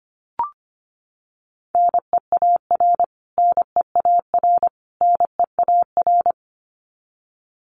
Morse code, heard as a keyed tone, about 25 wpm, 700 Hz. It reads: NEAR NEAR NEAR
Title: Top 500 Words Repeated 25wpm